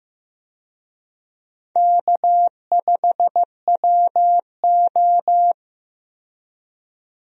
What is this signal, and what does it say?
Morse code, heard as a keyed tone, about 15 wpm, 700 Hz. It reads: K5WO
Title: USA Call Signs - 4 Characters - 15wpm